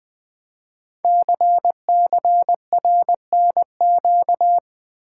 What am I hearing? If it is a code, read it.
Morse code, heard as a keyed tone, about 20 wpm, 700 Hz. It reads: CCRNQ
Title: Code Groups - 5 Letter:Numbers - 20wpm